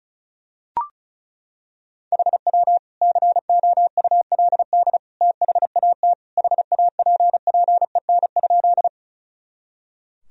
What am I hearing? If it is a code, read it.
Morse code, heard as a keyed tone, about 36 wpm, 700 Hz. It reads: HW COULD THAT HAPPEN?
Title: Sentences from Top 400 Words and Abbreviations 35wpm